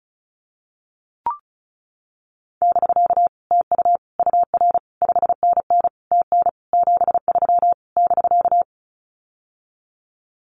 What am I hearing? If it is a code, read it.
Morse code, heard as a keyed tone, about 34 wpm, 700 Hz. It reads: <BK> TU UR 5NN TN 73 <BK>
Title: Common QSO Phrases Encore 35wpm